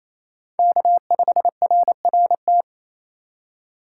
Morse code, heard as a keyed tone, about 34 wpm, 700 Hz. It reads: K5RRT
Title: Speed Racing - USA 1x3 Call Signs 28wpm